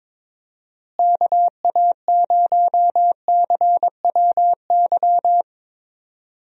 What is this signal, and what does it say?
Morse code, heard as a keyed tone, about 22 wpm, 700 Hz. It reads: KA0CWY
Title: USA Call Signs - 6 Characters - Repeated 3x - 22wpm